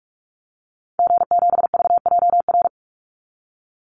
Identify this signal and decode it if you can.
Morse code, heard as a keyed tone, about 44 wpm, 700 Hz. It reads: G7VJR
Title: Call Signs Repeated 3x - 45wpm